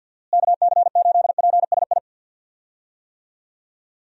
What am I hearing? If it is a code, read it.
Morse code, heard as a keyed tone, about 49 wpm, 700 Hz. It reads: KC8PSI
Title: USA Call Signs - 6 Characters - Repeated 3x - 50wpm